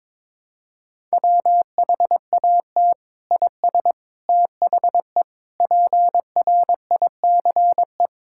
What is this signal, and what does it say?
Morse code, heard as a keyed tone, about 22 wpm, 700 Hz. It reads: WHAT IS THE PRICE
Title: Sentences from Top 1000 Words - Review Part 5 - 22wpm